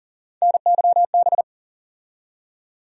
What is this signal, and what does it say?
Morse code, heard as a keyed tone, about 40 wpm, 700 Hz. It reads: NYB
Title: USA Call Sign Postfixes - 3 Letters 40wpm